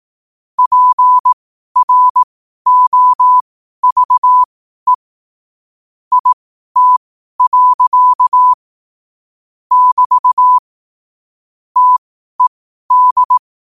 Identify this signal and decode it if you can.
Morse code, heard as a keyed tone, about 18 wpm, spaced as for 12 wpm, 1000 Hz. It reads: PROVE IT. <BT> TED
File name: qotdfiles / 12wpm